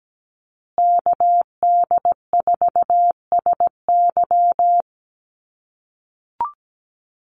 Morse code, heard as a keyed tone, about 17 wpm, 700 Hz. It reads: KD4SY
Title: USA Call Signs Repeated 3x 17wpm